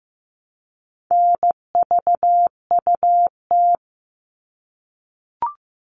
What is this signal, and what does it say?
Morse code, heard as a keyed tone, about 15 wpm, 700 Hz. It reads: NVUT
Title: All Cut Numbers - 4-Digit 15wpm